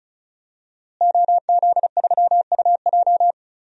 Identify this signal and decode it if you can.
Morse code, heard as a keyed tone, about 36 wpm, 700 Hz. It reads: OZ3UJ
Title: DXCC-100 Call Signs - 5 Characters - Repeated 2x - 35wpm